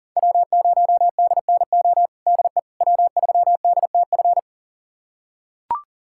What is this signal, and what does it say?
Morse code, heard as a keyed tone, about 37 wpm, 700 Hz. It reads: W0DNO DE W3DTF
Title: USA Callsign DE Callsign - 5 Characters - 40wpm